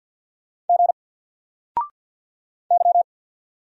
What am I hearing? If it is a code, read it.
Morse code, heard as a keyed tone, about 48 wpm, 700 Hz. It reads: G Y
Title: Single Character (Letter:Number:Punctuation:Prosign) 50wpm